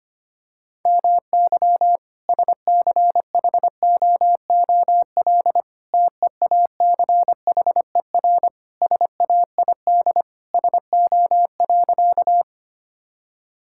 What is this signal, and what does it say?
Morse code, heard as a keyed tone, about 25 wpm, 700 Hz. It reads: MY SCHOOL TEACHER SAID SO.
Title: Sentences from Top 1000 Words - Review Part 6 - 25wpm